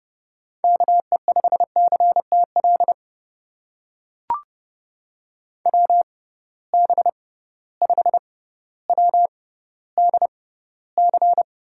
Morse code, heard as a keyed tone, about 29 wpm, spaced as for 15 wpm, 700 Hz. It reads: KE5CTL WB5WDC
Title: USA Call Signs - 6 Characters - 6x Character Spacing - 30wpm